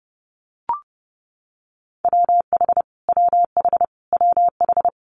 Morse code, heard as a keyed tone, about 32 wpm, 700 Hz. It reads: WH WH WH
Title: Multi-letter Phonemes Repeated 3x 30wpm